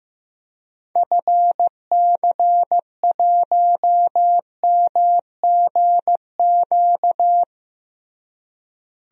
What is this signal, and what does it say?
Morse code, heard as a keyed tone, about 15 wpm, 700 Hz. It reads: FC1MGQ
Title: DXCC-100 Call Signs - 6 Characters - Repeated 2x - 15wpm